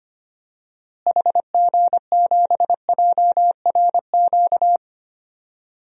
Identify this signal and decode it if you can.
Morse code, heard as a keyed tone, about 25 wpm, 700 Hz. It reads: HG7JRQ
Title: DXCC-100 Call Signs - 6 Characters - Repeated 2x - 25wpm